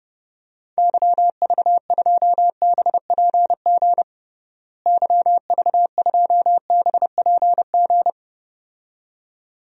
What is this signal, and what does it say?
Morse code, heard as a keyed tone, about 32 wpm, 700 Hz. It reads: YV2BPG YV2BPG
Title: DXCC-100 Call Signs - 6 Characters - Repeated 2x - 30wpm